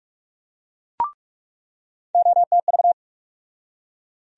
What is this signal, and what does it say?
Morse code, heard as a keyed tone, about 41 wpm, 700 Hz. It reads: OTV
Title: USA Call Sign Postfixes - 3 Letters 45wpm